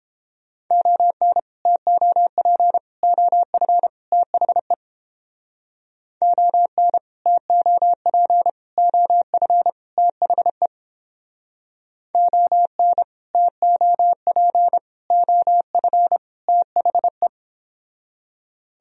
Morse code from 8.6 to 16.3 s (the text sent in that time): OF THE ON TOP OF